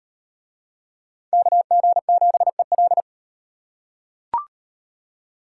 Morse code, heard as a keyed tone, about 38 wpm, 700 Hz. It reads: KG7EL